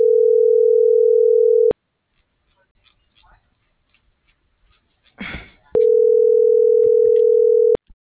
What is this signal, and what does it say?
Ringback tone (North America), 440 + 480 Hz.